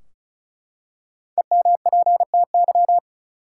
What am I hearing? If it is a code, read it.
Morse code, heard as a keyed tone, about 34 wpm, 700 Hz. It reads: EMPTY
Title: Five Letter Words - 35wpm